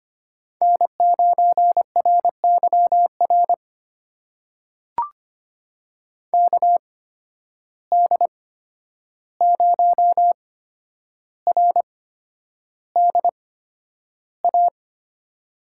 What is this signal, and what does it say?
Morse code, heard as a keyed tone, about 25 wpm, spaced as for 10 wpm, 700 Hz. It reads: N9RYR KD0RDA
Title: USA Call Signs - 8x Character Spacing - 25wpm